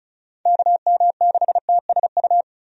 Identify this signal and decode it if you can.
Morse code, heard as a keyed tone, about 36 wpm, 700 Hz. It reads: KM6TSU
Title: USA Call Signs - 6 Characters - 35wpm